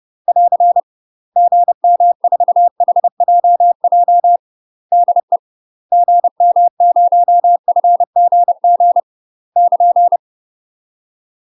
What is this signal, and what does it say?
Morse code, heard as a keyed tone, about 30 wpm, 700 Hz. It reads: <AR> GM4HJJ DE GM0FGG <KN>
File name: UK MASTER QSOs TRAINING / 30WPM / UK MASTER QSOs 0034